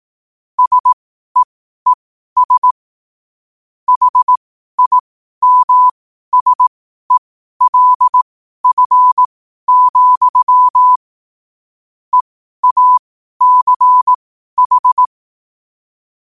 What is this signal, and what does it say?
Morse code, heard as a keyed tone, about 18 wpm, spaced as for 12 wpm, 1000 Hz. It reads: SEES HIMSELF, EACH
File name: qotdfiles / 12wpm